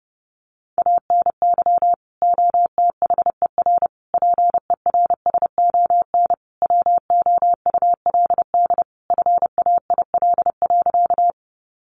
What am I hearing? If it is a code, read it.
Morse code, heard as a keyed tone, about 30 wpm, 700 Hz. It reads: ANY OTHER PERSON WOULD FAIL.
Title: Sentences from Top 1100 Words 30wpm